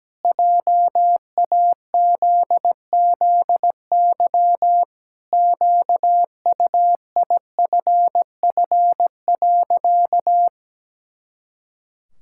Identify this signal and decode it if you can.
Morse code, heard as a keyed tone, about 17 wpm, 700 Hz. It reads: JAZZY QUIFF.
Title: Pangram Challenge v2 - 17wpm